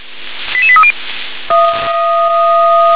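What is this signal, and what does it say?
Five-tone selective call (ZVEI), heard as quick runs of five tones, tones of about 70 ms. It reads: 88830
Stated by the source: ZVEI five-tone call 88743 received on SDR.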